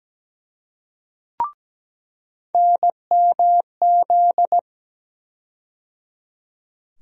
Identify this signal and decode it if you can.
Morse code, heard as a keyed tone, about 17 wpm, 700 Hz. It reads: NMZ
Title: USA Call Sign Postfixes - 3 Letters 17wpm